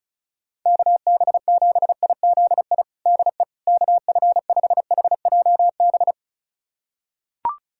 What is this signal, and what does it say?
Morse code, heard as a keyed tone, about 34 wpm, 700 Hz. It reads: KB7IZI DE KF5HJB
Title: USA Callsign DE Callsign - 6 Characters - 35wpm